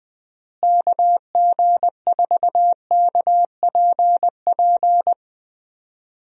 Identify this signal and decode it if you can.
Morse code, heard as a keyed tone, about 20 wpm, 700 Hz. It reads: KG4KPP
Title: USA Call Signs - 6 Characters - Repeated 3x - 20wpm